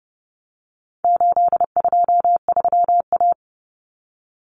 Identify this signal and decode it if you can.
Morse code, heard as a keyed tone, about 30 wpm, 700 Hz. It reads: 823A
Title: Common Cut Numbers - 4-Digit 30wpm